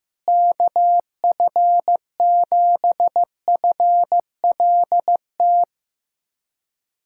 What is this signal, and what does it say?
Morse code, heard as a keyed tone, about 15 wpm, 700 Hz. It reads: KF7FLT
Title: USA Callsign DE Callsign 15wpm